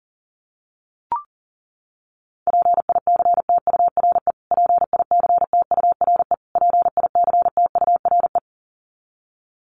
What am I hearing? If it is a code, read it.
Morse code, heard as a keyed tone, about 40 wpm, 700 Hz. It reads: PICTURE PICTURE PICTURE
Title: Top 200 Words Repeated 40wpm